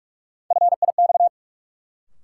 Morse code, heard as a keyed tone, about 42 wpm, 700 Hz. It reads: FIX
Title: Sentences from Top 1400 Words - New Word Drill - 45wpm